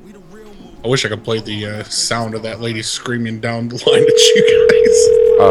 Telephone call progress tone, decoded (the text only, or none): ringback tone